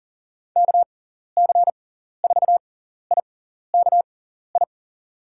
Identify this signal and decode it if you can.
Morse code, heard as a keyed tone, about 40 wpm, spaced as for 14 wpm, 700 Hz. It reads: KC4IKI
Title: USA Call Signs - 6 Characters - 6x Character Spacing - 40wpm